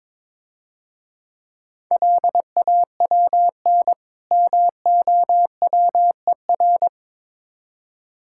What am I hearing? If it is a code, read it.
Morse code, heard as a keyed tone, about 24 wpm, 700 Hz. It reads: LAWN MOWER
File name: Easy Words - Speed-Racing with Hints - 22wpm